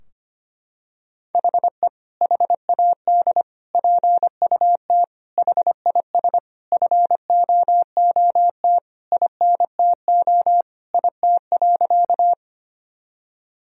Morse code, heard as a keyed tone, about 25 wpm, 700 Hz. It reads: HE HAD PUT HIS FOOT INTO IT.
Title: Sentences from Top 1000 Words - Review Part 3 - 25wpm